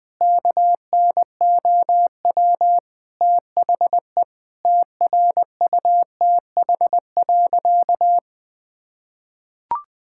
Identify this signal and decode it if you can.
Morse code, heard as a keyed tone, about 20 wpm, 700 Hz. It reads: KNOW THE TRUTH.
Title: Sentences with 7-Word Phrases - 20wpm